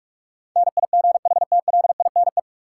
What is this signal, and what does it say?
Morse code, heard as a keyed tone, about 46 wpm, 700 Hz. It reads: NIGHTLINE
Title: Sets of 5 Words - New Word Drill - 45wpm